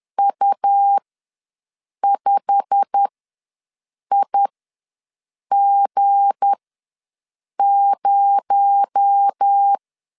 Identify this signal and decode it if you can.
Morse code, heard as a keyed tone, about 10 wpm, spaced as for 6 wpm, 800 Hz. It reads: U5IG0
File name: LEZIONI / 8 YJ45 / 8 YJ45 ws 10wpm 3843